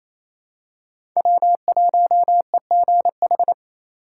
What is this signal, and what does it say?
Morse code, heard as a keyed tone, about 28 wpm, 700 Hz. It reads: W1EGH
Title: USA Call Signs - 2x Character Spacing - 28wpm